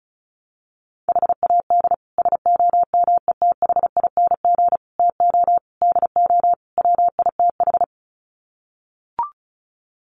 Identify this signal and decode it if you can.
Morse code, heard as a keyed tone, about 34 wpm, 700 Hz. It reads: HAD SOMETHING TO DO WITH
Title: Sets of 5 Words 35wpm